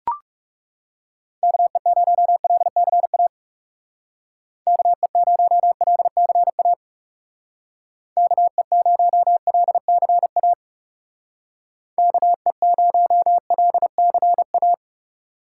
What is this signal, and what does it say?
Morse code, heard as a keyed tone, about 36 wpm, 700 Hz. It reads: KE0LCA KE0LCA KE0LCA KE0LCA
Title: Speed Racing - USA Call Signs 30wpm